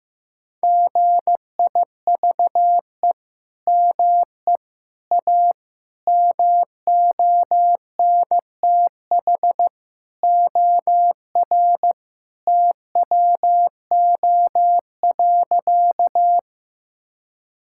Morse code, heard as a keyed tone, about 15 wpm, 700 Hz. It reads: GIVE ME A MONTH OR TWO.
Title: Sentences from Top 1000 Words - Review Part 5 - 15wpm